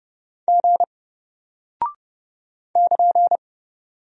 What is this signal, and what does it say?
Morse code, heard as a keyed tone, about 30 wpm, 700 Hz. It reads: G <KN>
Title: Single Character (Letter:Number:Punctuation:Prosign) Alt 30wpm